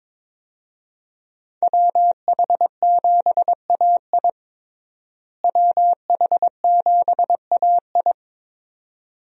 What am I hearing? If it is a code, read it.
Morse code, heard as a keyed tone, about 22 wpm, 700 Hz. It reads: WH7AI WH7AI
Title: USA Call Signs - 5 Characters - Repeated 2x - 22wpm